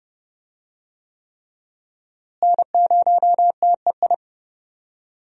Morse code, heard as a keyed tone, about 32 wpm, 700 Hz. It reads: N0TEI
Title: USA Call Signs - 5 Characters - Repeated 3x - 30wpm